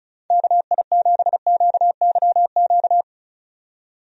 Morse code, heard as a keyed tone, about 36 wpm, 700 Hz. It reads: KI7QYQ
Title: USA Call Signs - 6 Characters - 35wpm